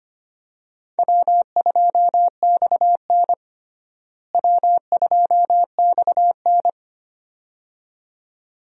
Morse code, heard as a keyed tone, about 25 wpm, 700 Hz. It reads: W2XN W2XN
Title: USA Call Signs - 4 Characters - Repeated 2x - 25wpm